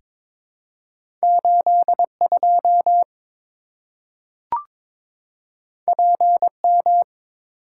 Morse code, heard as a keyed tone, about 22 wpm, 700 Hz. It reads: 82 PM
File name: practice / 22wpm / Two Letter-Number Combinations - 22wpm